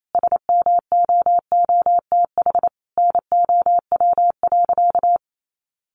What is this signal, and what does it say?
Morse code, heard as a keyed tone, about 28 wpm, 700 Hz. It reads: SMOOTH NOW.
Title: Sentences from Top 1400 Words 28wpm